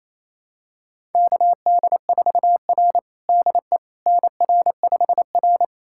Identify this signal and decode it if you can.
Morse code, heard as a keyed tone, about 27 wpm, 700 Hz. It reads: KD4R DE NR5R
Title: USA Callsign DE Callsign - 4 Characters - 28wpm